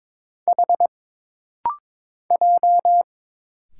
Morse code, heard as a keyed tone, about 22 wpm, 700 Hz. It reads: H J
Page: This rapid-fire timing is geared to push your ICR (Instant Character Recognition) skills.